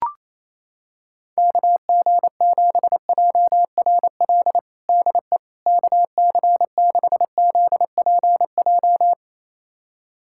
Morse code, heard as a keyed tone, about 28 wpm, 700 Hz. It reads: KG7JRL DE KC6ZPJ
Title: USA Callsign DE Callsign - 6 Characters - 28wpm